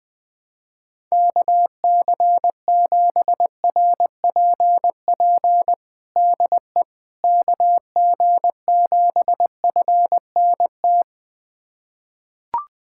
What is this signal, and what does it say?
Morse code, heard as a keyed tone, about 21 wpm, 700 Hz. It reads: KC7RPP DE KG7FNT
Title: USA Callsign DE Callsign - 6 Characters - 20wpm